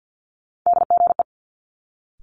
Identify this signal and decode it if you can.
Morse code, heard as a keyed tone, about 50 wpm, 700 Hz. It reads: DGE
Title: Multi-letter Phonemes Quick Fire 50wpm